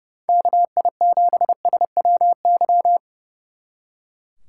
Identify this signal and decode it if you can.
Morse code, heard as a keyed tone, about 28 wpm, 700 Hz. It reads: KI7SWY
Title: Speed Racing - USA 2x3 Call Signs 30wpm